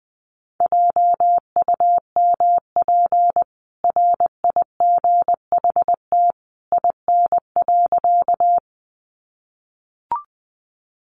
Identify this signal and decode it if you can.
Morse code, heard as a keyed tone, about 20 wpm, 700 Hz. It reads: JUMP RIGHT IN.
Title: Sentences from Top 1300 Words 20wpm